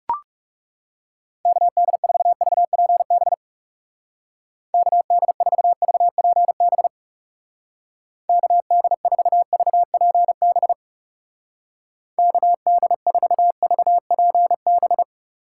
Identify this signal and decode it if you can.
Morse code, heard as a keyed tone, about 40 wpm, 700 Hz. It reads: KD4VPB KD4VPB KD4VPB KD4VPB